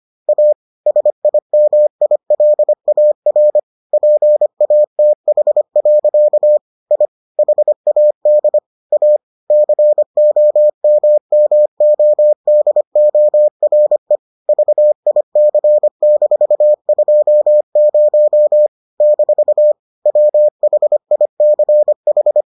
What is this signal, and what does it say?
Morse code, heard as a keyed tone, about 25 wpm, 600 Hz. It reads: A SIMILAR PATH. I HAD A COMMODORE VIC-20 - WHICH